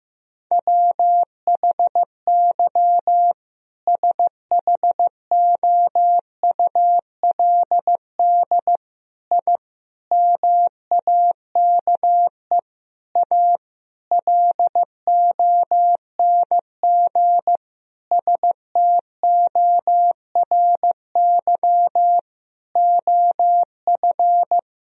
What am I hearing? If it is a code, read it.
Morse code, heard as a keyed tone, about 15 wpm, 700 Hz. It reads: WHY SHOULD I MAKE A LONG STORY OF
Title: Sentences from Top 1000 Words - Review Part 1 - 15wpm